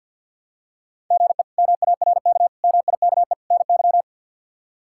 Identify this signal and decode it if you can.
Morse code, heard as a keyed tone, about 47 wpm, 700 Hz. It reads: GE MARK MIKE NY
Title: K1USN SST QSO Practice - 50wpm